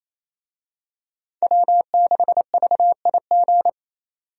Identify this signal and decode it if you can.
Morse code, heard as a keyed tone, about 28 wpm, 700 Hz. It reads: W6VIG
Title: USA Call Signs - 5 Characters - 28wpm